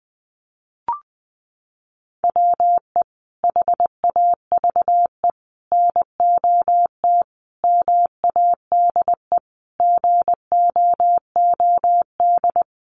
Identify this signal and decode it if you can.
Morse code, heard as a keyed tone, about 20 wpm, 700 Hz. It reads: WE HAVE NOT MADE GOOD
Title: Sentences from Top 1400 Words 20wpm